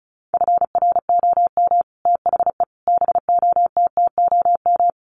Morse code, heard as a keyed tone, about 34 wpm, 700 Hz. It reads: FROM THE BOTTOM
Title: Sets of 5 Words Encore2 35wpm